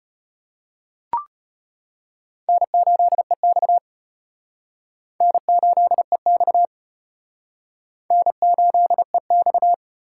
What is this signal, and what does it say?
Morse code, heard as a keyed tone, about 33 wpm, 700 Hz. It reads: N8EX N8EX N8EX